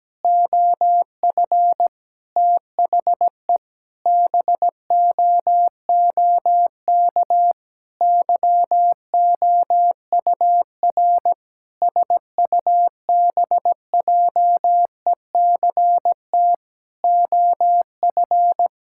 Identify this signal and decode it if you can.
Morse code, heard as a keyed tone, about 17 wpm, 700 Hz. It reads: OF THE BOOK YOUR SUBJECT OF
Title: Sentences from Top 1000 Words - Review Part 5 - 17wpm